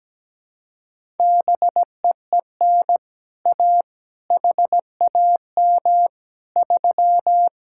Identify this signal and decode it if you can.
Morse code, heard as a keyed tone, about 17 wpm, 700 Hz. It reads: BEEN A HAM 3
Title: Speed-Racing Common QSO Phrases - 17wpm